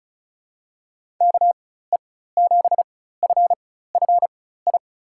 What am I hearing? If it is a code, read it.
Morse code, heard as a keyed tone, about 36 wpm, spaced as for 16 wpm, 700 Hz. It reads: KE7FFI
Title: USA Call Signs - 6 Characters - 4x Character Spacing - 35wpm